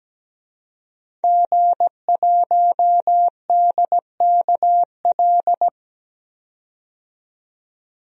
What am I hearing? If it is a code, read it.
Morse code, heard as a keyed tone, about 17 wpm, 700 Hz. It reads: G1DKL
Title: DXCC-100 Call Signs - 5 Characters - 17wpm